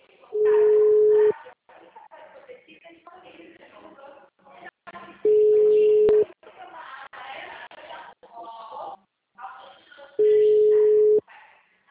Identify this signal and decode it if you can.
Ringback tone (Europe, 425 Hz), with speech.